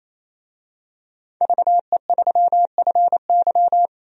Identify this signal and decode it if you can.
Morse code, heard as a keyed tone, about 28 wpm, 700 Hz. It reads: VE3FY
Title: DXCC-100 Call Signs - 5 Characters - 28wpm